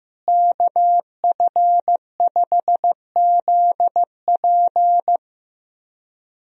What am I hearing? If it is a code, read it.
Morse code, heard as a keyed tone, about 16 wpm, 700 Hz. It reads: KF5ZP
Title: USA Call Signs - 5 Characters - Repeated 3x - 15wpm